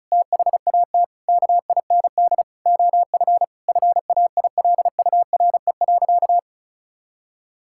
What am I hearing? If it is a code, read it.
Morse code, heard as a keyed tone, about 36 wpm, 700 Hz. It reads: THAT KIND OF FAILURE.
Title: Sentences from Top 1000 Words - Review Part 6 - 35wpm